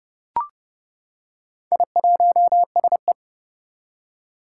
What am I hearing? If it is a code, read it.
Morse code, heard as a keyed tone, about 30 wpm, 700 Hz. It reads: I1SE
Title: Four Challenging Letter-Number Combinations 30wpm